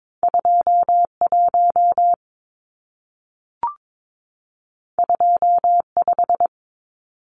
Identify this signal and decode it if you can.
Morse code, heard as a keyed tone, about 22 wpm, 700 Hz. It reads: 21 25
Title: Numbers - 2-Digit 22wpm